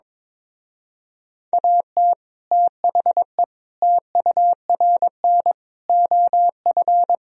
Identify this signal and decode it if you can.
Morse code, heard as a keyed tone, about 22 wpm, 700 Hz. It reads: AT THE TURN OF
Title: Speed-Racing - Sets of 5 Words 22wpm